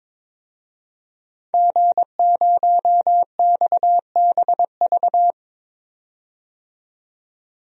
Morse code, heard as a keyed tone, about 22 wpm, 700 Hz. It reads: G0XBV
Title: DXCC-100 Call Signs - 5 Characters - 22wpm